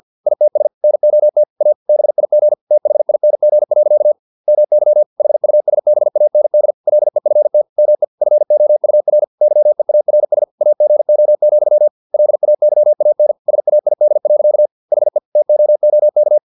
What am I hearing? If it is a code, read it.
Morse code, heard as a keyed tone, about 50 wpm, 600 Hz. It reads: ITS NOT A BIG THING. MY HUSBAND LEFT ME FOUR YEARS AGO, LAYAN SAID. HE TOOK
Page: Never 49wpm